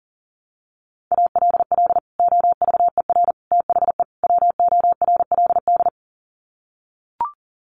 Morse code, heard as a keyed tone, about 40 wpm, 700 Hz. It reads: ALL OVER THE WORLD
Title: Sets of 4 Words 40wpm